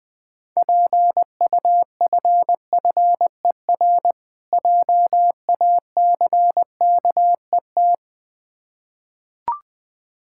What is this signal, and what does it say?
Morse code, heard as a keyed tone, about 20 wpm, 690 Hz. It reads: PUFFER JACKET
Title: Difficult Words with Hints - 20wpm